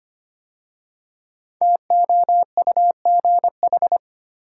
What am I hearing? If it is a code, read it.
Morse code, heard as a keyed tone, about 25 wpm, 700 Hz. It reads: TOUGH
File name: 5 Letter Words - First 3 Letters Given - 25wpm